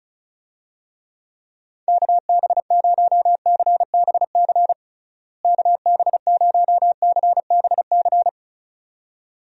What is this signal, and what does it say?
Morse code, heard as a keyed tone, about 34 wpm, 700 Hz. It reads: KB0CBC KB0CBC
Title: USA Call Signs - 6 Characters - Repeated 2x - 35wpm